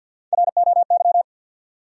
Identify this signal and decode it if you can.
Morse code, heard as a keyed tone, about 53 wpm, 700 Hz. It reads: ROY
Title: USA Common Names - Max 5 Letters - 50wpm